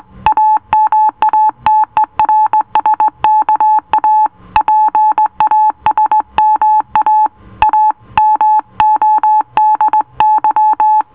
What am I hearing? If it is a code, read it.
Morse code, heard as a keyed tone, about 18 wpm, 860 Hz. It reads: AMATERSKA PASMA A MODY